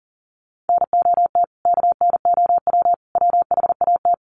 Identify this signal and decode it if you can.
Morse code, heard as a keyed tone, about 40 wpm, 700 Hz. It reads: NOT KNOW WHAT
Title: Sets of 3 Words 40wpm